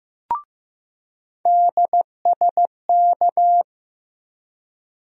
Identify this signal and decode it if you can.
Morse code, heard as a keyed tone, about 15 wpm, 700 Hz. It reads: DSK
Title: USA Call Sign Postfixes - 3 Letters 15wpm